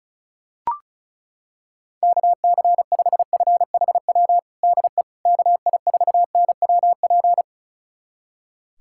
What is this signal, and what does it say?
Morse code, heard as a keyed tone, about 36 wpm, 700 Hz. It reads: KC5FHW DE KI4NWP
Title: USA Callsign DE Callsign 35wpm